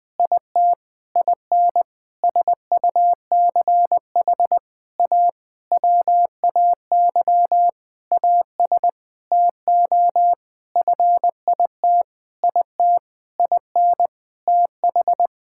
Morse code, heard as a keyed tone, about 20 wpm, 700 Hz. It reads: IT IN SUCH A WAY AS TO FIT IT IN TH
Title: Sentences with 6-Word Phrases - 20wpm